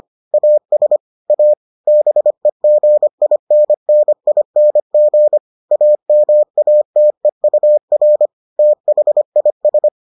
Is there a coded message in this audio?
Yes: Morse code from 0.3 to 9.9 s, heard as a keyed tone, about 25 wpm, 600 Hz. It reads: AS A BEGINNING AMATEUR THIS